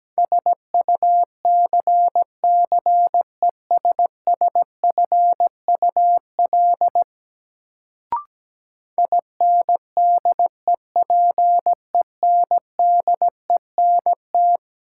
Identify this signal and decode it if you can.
Morse code, heard as a keyed tone, about 17 wpm, 700 Hz. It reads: SUCCESSFUL INDEPENDENT
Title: Sentences from Top 900 Words - New Word Drill - 17wpm